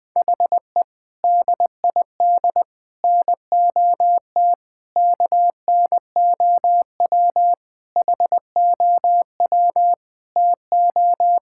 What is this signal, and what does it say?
Morse code, heard as a keyed tone, about 20 wpm, 700 Hz. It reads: HE DID NOT KNOW HOW TO
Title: Sentences with 5-Word Phrases - Spoken Beginning - 20wpm